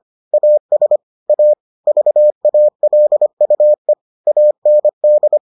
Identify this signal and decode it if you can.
Morse code, heard as a keyed tone, about 25 wpm, 600 Hz. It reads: AS A VALUE AND